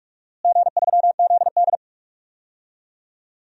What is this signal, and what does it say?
Morse code, heard as a keyed tone, about 46 wpm, 700 Hz. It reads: G3ZD
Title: DXCC-100 Call Signs - 4 Characters - Repeated 3x - 45wpm